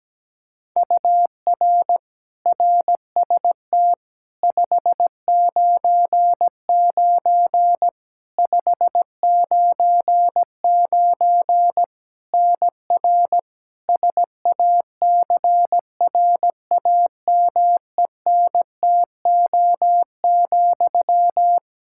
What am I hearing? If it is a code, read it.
Morse code, heard as a keyed tone, about 17 wpm, 700 Hz. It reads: UR RST 599 599 NR SACRAMENTO,
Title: K1USN SST QSO Practice 17wpm